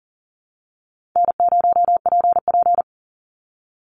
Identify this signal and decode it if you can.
Morse code, heard as a keyed tone, about 40 wpm, 700 Hz. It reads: N0PP
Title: Easy Call Signs Repeated 2x - 40wpm